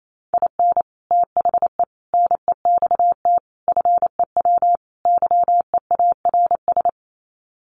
Morse code, heard as a keyed tone, about 28 wpm, 690 Hz. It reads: IN THE NEXT FEW YEARS
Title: Sets of 5 Words Encore 28wpm